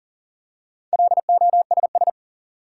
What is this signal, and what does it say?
Morse code, heard as a keyed tone, about 40 wpm, 700 Hz. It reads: LOSS
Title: Four Letter Words - 40wpm